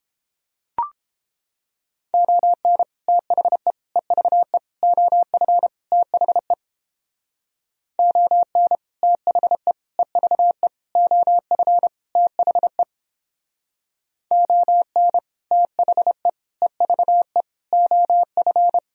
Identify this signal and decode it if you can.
Morse code, heard as a keyed tone, about 29 wpm, 700 Hz. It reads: ON THE EVE OF THE ON THE EVE OF THE ON THE EVE OF